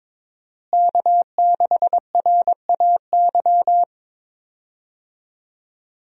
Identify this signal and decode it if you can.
Morse code, heard as a keyed tone, about 22 wpm, 700 Hz. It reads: K6RAY
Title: USA Call Signs - 5 Characters - Repeated 2x - 22wpm